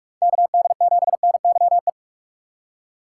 Morse code, heard as a keyed tone, about 43 wpm, 700 Hz. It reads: KD7NYE
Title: USA Call Signs - 6 Characters - 4x Character Spacing - 45wpm